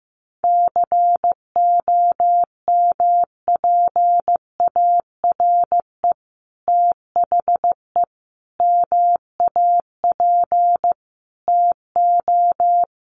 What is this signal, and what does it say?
Morse code, heard as a keyed tone, about 15 wpm, 700 Hz. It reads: COMPARE THE MAP TO
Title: Sentences from Top 1300 Words 15wpm